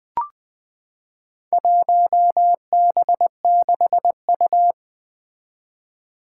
Morse code, heard as a keyed tone, about 21 wpm, 700 Hz. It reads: 1B6U
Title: Four Challenging Letter-Number Combinations 20wpm